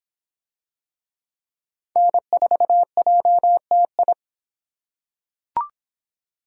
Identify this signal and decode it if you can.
Morse code, heard as a keyed tone, about 26 wpm, 700 Hz. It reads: N4JTI